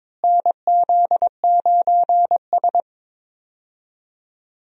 Morse code, heard as a keyed tone, about 22 wpm, 700 Hz. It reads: NZ9S
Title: USA Call Signs - 4 Characters - Repeated 3x - 22wpm